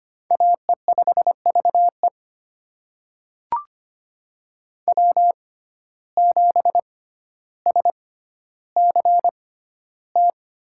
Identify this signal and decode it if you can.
Morse code, heard as a keyed tone, about 25 wpm, spaced as for 13 wpm, 700 Hz. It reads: AE5VE W7SCT
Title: USA Call Signs - 5 Characters - 6x Character Spacing - 25wpm